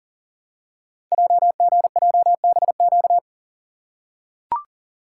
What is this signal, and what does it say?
Morse code, heard as a keyed tone, about 38 wpm, 700 Hz. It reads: JGJBQ
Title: Code Groups - 5 Letters - 40wpm